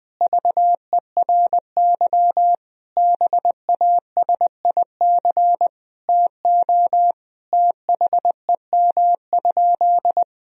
Morse code, heard as a keyed tone, about 20 wpm, 700 Hz. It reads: VERY BASIC TO THEM?
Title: Sentences from Top 1000 Words - Review Part 3 - 20wpm